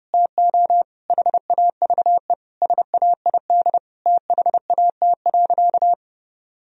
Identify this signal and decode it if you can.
Morse code, heard as a keyed tone, about 30 wpm, 700 Hz. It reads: TO HAVE SAID THAT.
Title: Sentences from Top 1000 Words - Review Part 1 - 30wpm